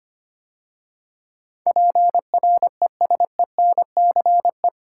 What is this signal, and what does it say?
Morse code, heard as a keyed tone, about 25 wpm, 700 Hz. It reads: PRESENCE
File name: 8 Letter Words - First 3 Letters Given - 25wpm